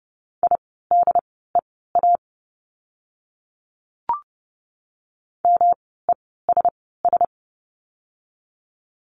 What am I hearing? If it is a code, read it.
Morse code, heard as a keyed tone, about 30 wpm, spaced as for 17 wpm, 700 Hz. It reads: IDEA MESS
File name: course / 30:10wpm / 108 Lesson 18f - Letter m - Words 30wpm